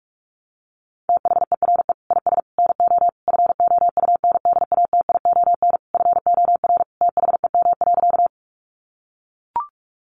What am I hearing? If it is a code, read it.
Morse code, heard as a keyed tone, about 44 wpm, 700 Hz. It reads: THERE IS NO FOUNDATION FOR THEM.
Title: Sentences from Top 1200 Words 45wpm